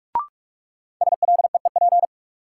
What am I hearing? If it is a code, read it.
Morse code, heard as a keyed tone, about 44 wpm, 700 Hz. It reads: SLEEP
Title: Sentences from Top 500 Words - New Word Drill - 45wpm